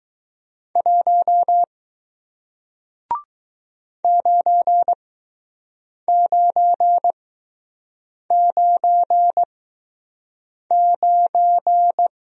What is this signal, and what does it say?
Morse code, heard as a keyed tone, about 20 wpm, spaced as for 6 wpm, 700 Hz. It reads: 1 9999